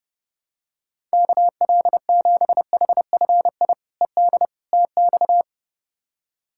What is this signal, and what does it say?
Morse code, heard as a keyed tone, about 31 wpm, 700 Hz. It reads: KL7HFI ED TX
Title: K1USN SST QSO Practice - 30wpm